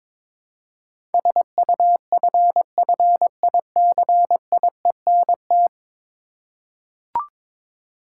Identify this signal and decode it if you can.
Morse code, heard as a keyed tone, about 21 wpm, 700 Hz. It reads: SUFFICIENT
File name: practice / 22wpm / 10 Letter Words - First 3 Letters Given - 22wpm